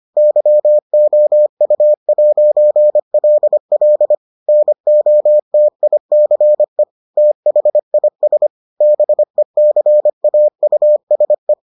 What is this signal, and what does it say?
Morse code, heard as a keyed tone, about 25 wpm, 600 Hz. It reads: YOU'LL NOTICE THIS BECAUSE